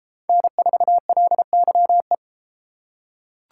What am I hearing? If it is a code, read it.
Morse code, heard as a keyed tone, about 32 wpm, 700 Hz. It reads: N4LYE